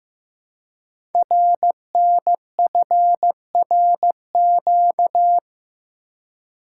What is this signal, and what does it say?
Morse code, heard as a keyed tone, about 15 wpm, 700 Hz. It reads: RNFRQ
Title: Code Groups - 5 Letters - 15wpm